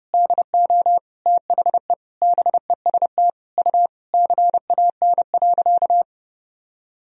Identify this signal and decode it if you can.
Morse code, heard as a keyed tone, about 30 wpm, 700 Hz. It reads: DO THE BEST U CAN.
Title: Sentences from Top 500 Words and Abbreviations 30wpm